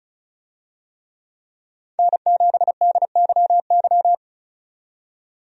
Morse code, heard as a keyed tone, about 34 wpm, 700 Hz. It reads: N7DYY
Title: USA Call Signs - 5 Characters - 35wpm